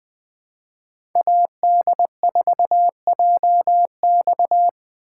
Morse code, heard as a keyed tone, about 20 wpm, 700 Hz. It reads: AD4JX
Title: 2023 POTA Hunter Call Signs - 20wpm